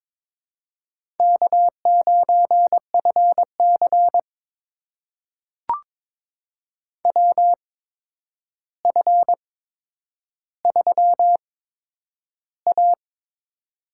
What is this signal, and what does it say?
Morse code, heard as a keyed tone, about 22 wpm, spaced as for 10 wpm, 700 Hz. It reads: K9FC WF3A
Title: USA Call Signs - 4 Characters - 8x Character Spacing - 22wpm